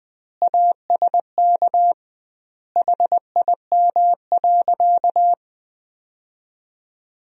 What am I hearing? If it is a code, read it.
Morse code, heard as a keyed tone, about 20 wpm, 700 Hz. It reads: ASK HIM.
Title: 2x Word-Spacing - Sentences from Top 300 Words - 20wpm